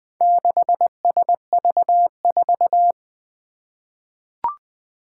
Morse code, heard as a keyed tone, about 20 wpm, 700 Hz. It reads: 6SV4
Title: Four Challenging Letter-Number Combinations 20wpm